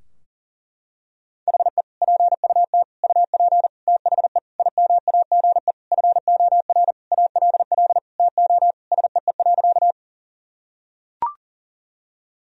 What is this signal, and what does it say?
Morse code, heard as a keyed tone, about 40 wpm, 700 Hz. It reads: HE PUT UP THE IMAGE FOR ALL TO SEE.
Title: Sentences from Top 1000 Words - Review Part 4 - 40wpm